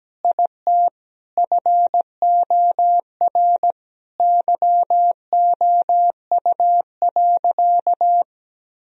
Morse code, heard as a keyed tone, about 17 wpm, 700 Hz. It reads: IT FOR YOU.
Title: Sentences from Top 1000 Words - Review Part 2 - 17wpm